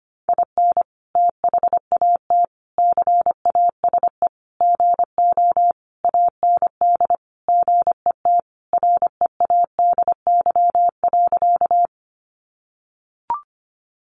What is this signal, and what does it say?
Morse code, heard as a keyed tone, about 25 wpm, 700 Hz. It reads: IN THAT CASE GO AND GET READY.
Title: Sentences from Top 700 Words 25wpm